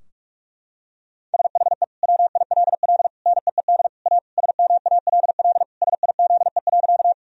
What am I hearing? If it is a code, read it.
Morse code, heard as a keyed tone, about 42 wpm, 700 Hz. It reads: SHE WILL NEED A SMALL SIZE.
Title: Sentences from Top 1000 Words - Review Part 2 - 45wpm